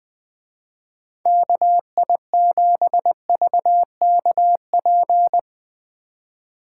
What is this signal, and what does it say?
Morse code, heard as a keyed tone, about 20 wpm, 700 Hz. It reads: KI7VKP
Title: Speed Racing - USA 2x3 Call Signs 20wpm